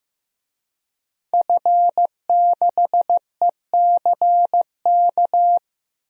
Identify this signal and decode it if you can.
Morse code, heard as a keyed tone, about 16 wpm, 700 Hz. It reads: F6ECK
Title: DXCC-100 Call Signs - 5 Characters - 15wpm